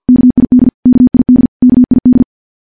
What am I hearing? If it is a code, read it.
Morse code, heard as a keyed tone, about 50 wpm, 250 Hz. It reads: KID KID KID
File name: morse / morse-training-50WPM